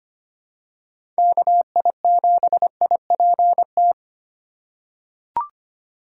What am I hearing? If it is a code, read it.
Morse code, heard as a keyed tone, about 25 wpm, 700 Hz. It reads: KI7IPT
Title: USA Call Signs - 6 Characters - 4x Character Spacing - 25wpm